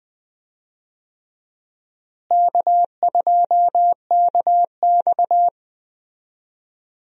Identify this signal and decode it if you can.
Morse code, heard as a keyed tone, about 20 wpm, 700 Hz. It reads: K2KX
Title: USA Call Signs - 4 Characters - Repeated 3x - 20wpm